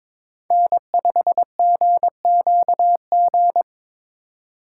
Morse code, heard as a keyed tone, about 25 wpm, 700 Hz. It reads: N5GQG
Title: Speed Racing - USA 1x3 Call Signs 22wpm